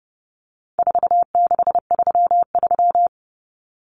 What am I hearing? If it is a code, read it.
Morse code, heard as a keyed tone, about 30 wpm, 700 Hz. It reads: 4633
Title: Common Cut Numbers - 4-Digit 30wpm